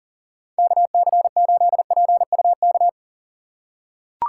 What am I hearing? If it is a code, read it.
Morse code, heard as a keyed tone, about 40 wpm, 700 Hz. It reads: KC8PUK
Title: USA Call Signs - 6 Characters - 2x Character Spacing - 40wpm